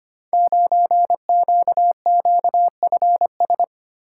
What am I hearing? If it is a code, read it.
Morse code, heard as a keyed tone, about 25 wpm, 700 Hz. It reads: 9QQFS
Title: Code Groups - 5 Letter:Numbers - 25wpm